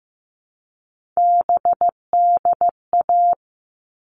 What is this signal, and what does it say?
Morse code, heard as a keyed tone, about 15 wpm, 700 Hz. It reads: BDA
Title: All Cut Numbers - 3-Digit 15wpm